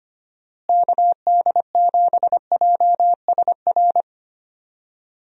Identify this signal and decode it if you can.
Morse code, heard as a keyed tone, about 25 wpm, 700 Hz. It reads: KD7JSR
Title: USA Call Signs - 6 Characters - Repeated 3x - 25wpm